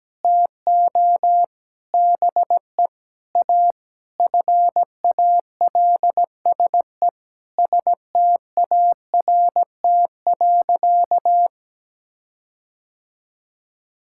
Morse code, heard as a keyed tone, about 17 wpm, 700 Hz. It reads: TO BE A FALSE START.
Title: Sentences from Top 1600 Words 17wpm